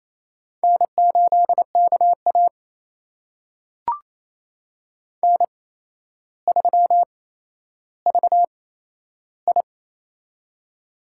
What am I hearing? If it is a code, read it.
Morse code, heard as a keyed tone, about 28 wpm, spaced as for 12 wpm, 700 Hz. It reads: N8KA N3VI
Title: USA Call Signs - 4 Characters - 8x Character Spacing - 28wpm